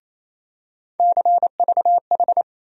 Morse code, heard as a keyed tone, about 28 wpm, 700 Hz. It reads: CVH
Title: USA Call Sign Postfixes - 3 Letters 28wpm